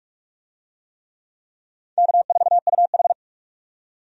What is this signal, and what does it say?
Morse code, heard as a keyed tone, about 46 wpm, 700 Hz. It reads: K4UH
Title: USA Call Signs - 4 Characters - 45wpm